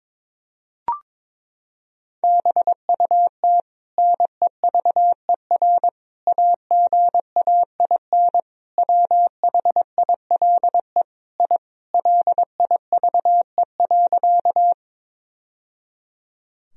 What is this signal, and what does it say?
Morse code, heard as a keyed tone, about 21 wpm, 700 Hz. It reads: BUT NEVER AGAIN WHILE I LIVE.
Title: Sentences from Top 1000 Words - Review Part 1 - 22wpm